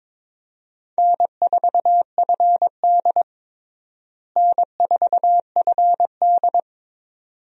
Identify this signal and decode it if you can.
Morse code, heard as a keyed tone, about 22 wpm, 700 Hz. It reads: N4FD N4FD
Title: USA Call Signs - 4 Characters - Repeated 2x - 22wpm